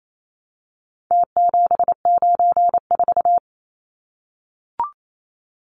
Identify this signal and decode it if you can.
Morse code, heard as a keyed tone, about 28 wpm, 700 Hz. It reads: T794
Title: Numbers - Contest Style - 4-Digit 28wpm